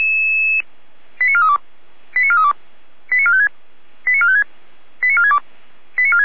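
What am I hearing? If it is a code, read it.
Five-tone selective call (ZVEI), heard as quick runs of five tones, tones of about 70 ms. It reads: 89432 89432 89456 89456 89462
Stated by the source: ZVEI five-tone selective call received on SDR.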